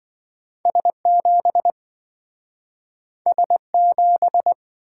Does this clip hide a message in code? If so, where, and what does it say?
Yes: Morse code from 0.7 to 4.5 s, heard as a keyed tone, about 27 wpm, 700 Hz. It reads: S7 S7